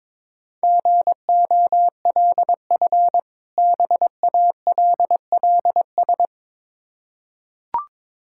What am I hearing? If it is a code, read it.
Morse code, heard as a keyed tone, about 22 wpm, 690 Hz. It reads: GOLF BALLS
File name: Difficult Words with Hints - 22wpm